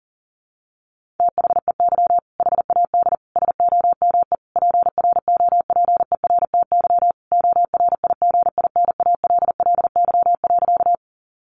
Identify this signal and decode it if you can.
Morse code, heard as a keyed tone, about 40 wpm, 700 Hz. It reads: THEY HAD SOME PROPERTY ORIGINALLY.
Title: Sentences from Top 1500 Words 40wpm